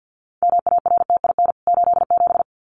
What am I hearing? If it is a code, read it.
Morse code, heard as a keyed tone, about 50 wpm, 700 Hz. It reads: MARTIN 87
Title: CWT Exchange 50wpm